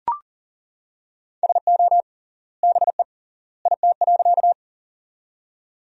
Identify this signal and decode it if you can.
Morse code, heard as a keyed tone, about 39 wpm, 700 Hz. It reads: SO BE IT.
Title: 3x Word-Spacing - Sentences from Top 100 Words - 40wpm